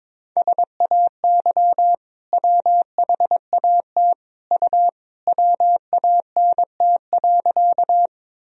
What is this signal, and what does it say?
Morse code, heard as a keyed tone, about 22 wpm, 700 Hz. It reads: SAY WHAT U WANT.
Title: Sentences from Top 200 Words and Abbreviations 22wpm